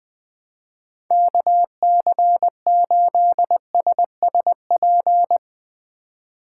Morse code, heard as a keyed tone, about 20 wpm, 700 Hz. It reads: KC8SSP
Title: USA Call Signs - 6 Characters - 2x Character Spacing - 20wpm